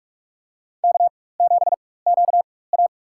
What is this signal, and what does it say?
Morse code, heard as a keyed tone, about 46 wpm, spaced as for 21 wpm, 690 Hz. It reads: K7QA
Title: USA Call Signs - 4 Characters - 4x Character Spacing - 45wpm